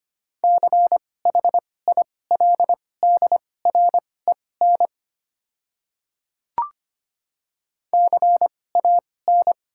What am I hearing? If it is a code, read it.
Morse code, heard as a keyed tone, about 25 wpm, spaced as for 18 wpm, 700 Hz. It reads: CHILDREN CAN
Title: 2x Character Spacing - Top 300 Words - 25wpm